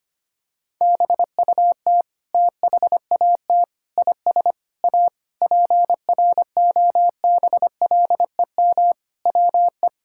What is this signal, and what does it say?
Morse code, heard as a keyed tone, about 25 wpm, 700 Hz. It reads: BUT THAT IS A PROBLEM WE
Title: Sentences from Top 1000 Words - Review Part 6 - 25wpm